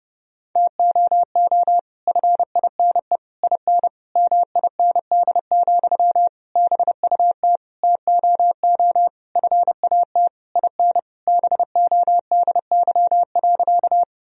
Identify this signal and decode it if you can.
Morse code, heard as a keyed tone, about 30 wpm, 700 Hz. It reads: TOO FINE IN MIND, BUT TOO FAT IN BODY.
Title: Sentences from Top 1000 Words - Review Part 4 - 30wpm